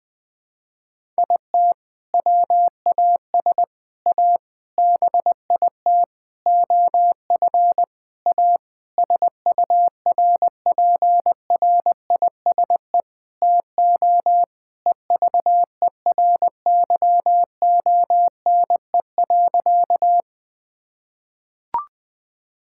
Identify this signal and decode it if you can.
Morse code, heard as a keyed tone, about 20 wpm, 700 Hz. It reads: IT WAS A BIT OF A SURPRISE TO EVERYONE.
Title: Sentences with 6-Word Phrases - Spoken Beginning - 20wpm